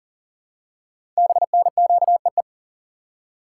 Morse code, heard as a keyed tone, about 42 wpm, 690 Hz. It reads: BNQEE
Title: Code Groups - 5 Letters - 40wpm